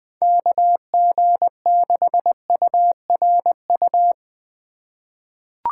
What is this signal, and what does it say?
Morse code, heard as a keyed tone, about 20 wpm, 700 Hz. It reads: KG6URU
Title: USA Call Signs - 4x Character Spacing - 20wpm